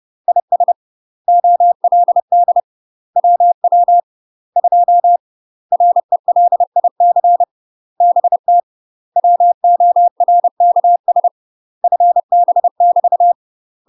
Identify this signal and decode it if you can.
Morse code, heard as a keyed tone, about 30 wpm, 700 Hz. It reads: IS OLD WW 2 RELIC BT WORKS FB <BT>
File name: UK MASTER QSOs TRAINING / 30WPM / UK MASTER QSOs 0041